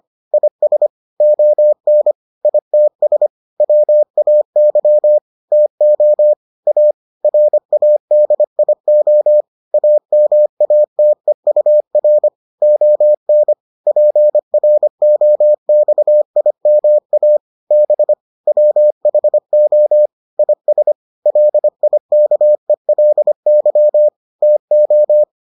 Morse code, heard as a keyed tone, about 25 wpm, 600 Hz. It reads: IS ON ITS WAY TO A RADIO AMATEUR ON PROXIMA B WHO IS LIKELY TO